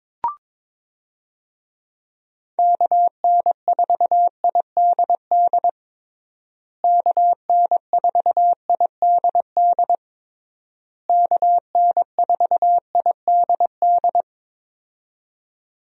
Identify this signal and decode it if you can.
Morse code, heard as a keyed tone, about 22 wpm, 700 Hz. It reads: KN4IDD KN4IDD KN4IDD
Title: USA Call Signs - 6 Characters - Repeated 3x - 22wpm